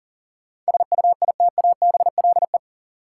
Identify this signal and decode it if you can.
Morse code, heard as a keyed tone, about 41 wpm, 700 Hz. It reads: SUITABLE
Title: Sentences from Top 1400 Words - New Word Drill - 40wpm